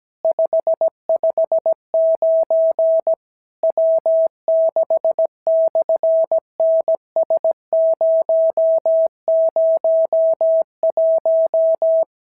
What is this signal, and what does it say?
Morse code, heard as a keyed tone, about 17 wpm, 650 Hz. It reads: 559 W6/NS001
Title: SOTA QSO Practice 17wpm